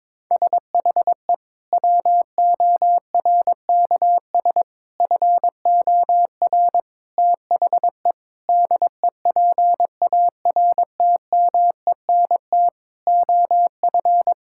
Morse code, heard as a keyed tone, about 22 wpm, 700 Hz. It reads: SHE WORKS FOR THE DEPARTMENT OF
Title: Sentences from Top 1000 Words - Review Part 5 - 22wpm